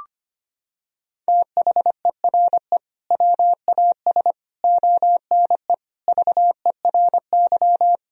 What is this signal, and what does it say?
Morse code, heard as a keyed tone, about 25 wpm, 700 Hz. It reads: THERE WAS ONE VERY
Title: Sentences from Top 1000 Words - Review Part 3 - 25wpm